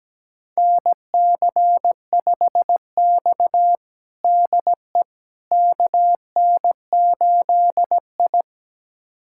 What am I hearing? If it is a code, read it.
Morse code, heard as a keyed tone, about 17 wpm, 700 Hz. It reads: NC5X DE KN8I
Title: USA Callsign DE Callsign - 4 Characters - 17wpm